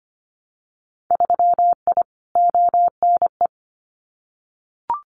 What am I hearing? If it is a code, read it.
Morse code, heard as a keyed tone, about 25 wpm, 690 Hz. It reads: 3I ONE
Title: Winter Field Day Exchanges Encore 25wpm